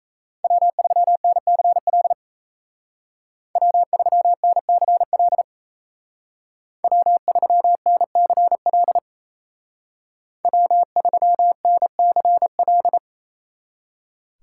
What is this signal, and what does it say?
Morse code, heard as a keyed tone, about 36 wpm, 700 Hz. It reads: W3NCL W3NCL W3NCL W3NCL